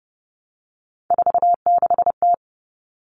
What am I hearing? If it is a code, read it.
Morse code, heard as a keyed tone, about 30 wpm, 700 Hz. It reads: 46T
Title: Common Cut Numbers - 3-Digit 30wpm